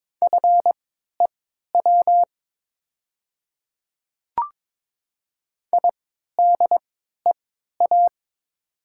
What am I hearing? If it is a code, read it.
Morse code, heard as a keyed tone, about 22 wpm, spaced as for 12 wpm, 700 Hz. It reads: FEW IDEA
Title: 3x Character Spacing - Top 500 Words - 22wpm